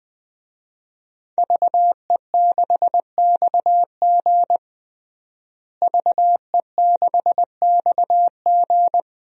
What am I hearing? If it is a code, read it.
Morse code, heard as a keyed tone, about 21 wpm, 700 Hz. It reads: VE6XG VE6XG
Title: DXCC-100 Call Signs - 5 Characters - Repeated 2x - 20wpm